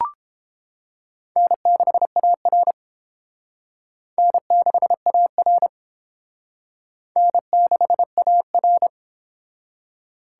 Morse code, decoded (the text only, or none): N6AR N6AR N6AR